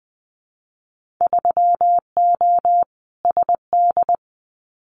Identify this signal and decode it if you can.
Morse code, heard as a keyed tone, about 20 wpm, 700 Hz. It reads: 3O SD
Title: Winter Field Day Exchanges 20wpm